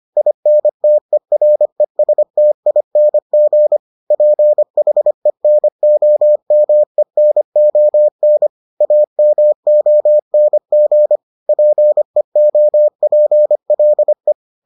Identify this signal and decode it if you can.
Morse code, heard as a keyed tone, about 25 wpm, 600 Hz. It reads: INTERESTING PHENOMENON AMONG PEOPLE